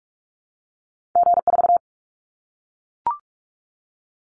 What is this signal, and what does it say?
Morse code, heard as a keyed tone, about 44 wpm, 700 Hz. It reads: G4
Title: Callsign Prefixes for Top 25 DXCC - Repeated 3x - 45wpm